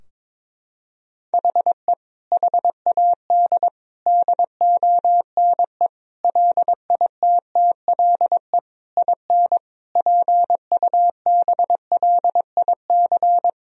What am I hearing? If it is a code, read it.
Morse code, heard as a keyed tone, about 22 wpm, 700 Hz. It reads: HE HAD DONE LITTLE IN PUBLIC
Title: Sentences from Top 1000 Words - Review Part 3 - 22wpm